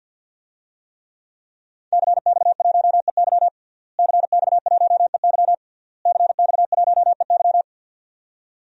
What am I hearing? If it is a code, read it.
Morse code, heard as a keyed tone, about 47 wpm, 700 Hz. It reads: CX1EY CX1EY CX1EY
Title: DXCC-100 Call Signs - 5 Characters - Repeated 3x - 50wpm